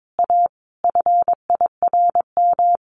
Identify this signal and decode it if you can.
Morse code, heard as a keyed tone, about 22 wpm, 700 Hz. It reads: A FIRM
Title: Sentences from Top 1400 Words 22wpm